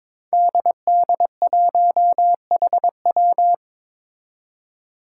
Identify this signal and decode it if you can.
Morse code, heard as a keyed tone, about 22 wpm, 700 Hz. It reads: DD1HW
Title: Five Challenging Letter-Number Combinations 22wpm